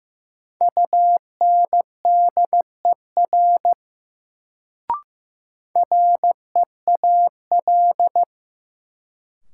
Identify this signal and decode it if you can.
Morse code, heard as a keyed tone, about 15 wpm, 700 Hz. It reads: UNDER REAL
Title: Sentences from Top 300 Words - New Word Drill - 15wpm